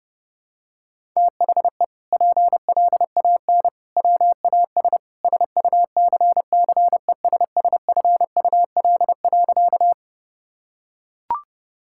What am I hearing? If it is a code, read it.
Morse code, heard as a keyed tone, about 30 wpm, 700 Hz. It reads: THE PLAN WAS SUCCESSFUL.
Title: Sentences from Top 1000 Words - Review Part 6 - 30wpm